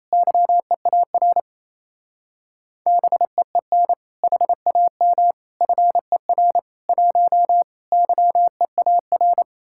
Morse code, heard as a keyed tone, about 36 wpm, 700 Hz. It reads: YEAR BEEN HAM FER 1 YEAR